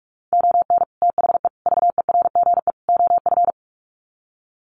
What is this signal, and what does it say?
Morse code, heard as a keyed tone, about 44 wpm, 700 Hz. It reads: ON THE VERGE OF
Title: Sets of 4 Words 45wpm